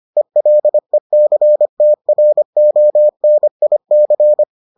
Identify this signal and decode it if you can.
Morse code, heard as a keyed tone, about 25 wpm, 600 Hz. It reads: ELECTRONIC